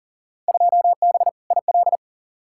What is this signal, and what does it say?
Morse code, heard as a keyed tone, about 38 wpm, 700 Hz. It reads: 2B IL
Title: Field Day Exchanges Encore - 40wpm